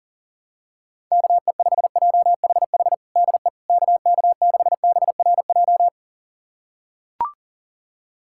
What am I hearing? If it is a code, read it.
Morse code, heard as a keyed tone, about 40 wpm, 700 Hz. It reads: KE5JHH DE KK6BRJ
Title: USA Callsign DE Callsign - 6 Characters - 40wpm